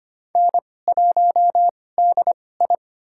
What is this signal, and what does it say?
Morse code, heard as a keyed tone, about 25 wpm, spaced as for 18 wpm, 700 Hz. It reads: N1DI
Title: USA Call Signs - 4 Characters - 2x Character Spacing - 25wpm